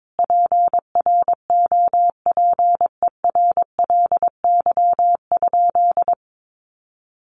Morse code, heard as a keyed tone, about 22 wpm, 700 Hz. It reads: PROPERLY?
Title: Sentences from Top 1100 Words 22wpm